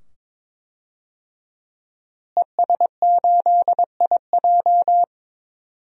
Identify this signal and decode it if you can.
Morse code, heard as a keyed tone, about 22 wpm, 700 Hz. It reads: ES8IJ
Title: DXCC-100 Call Signs - 5 Characters - Repeated 2x - 22wpm